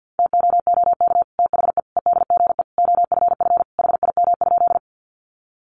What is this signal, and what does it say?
Morse code, heard as a keyed tone, about 50 wpm, 700 Hz. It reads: TOOK THE EDGE OFF HIM?
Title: Sentences from Top 1300 Words 50wpm